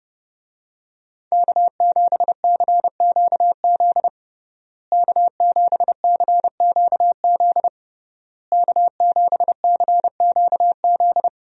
Morse code, heard as a keyed tone, about 31 wpm, 700 Hz. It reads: K7CQZ K7CQZ K7CQZ
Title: USA Call Signs - 5 Characters - Repeated 3x - 30wpm